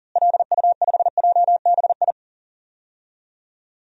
Morse code, heard as a keyed tone, about 40 wpm, 700 Hz. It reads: LU5JBI
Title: DXCC-100 Call Signs - 6 Characters - Repeated 3x - 40wpm